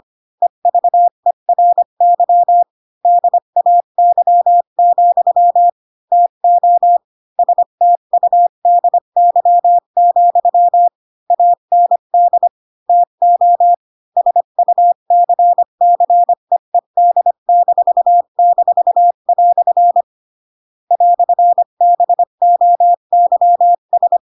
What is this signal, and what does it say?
Morse code, heard as a keyed tone, about 25 wpm, 700 Hz. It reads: EVERY DAY, TO STUDY, AND TO SUCCEED--" "BOYS
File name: Pinocchio / Pinocchio ch8 20 25 ik1hge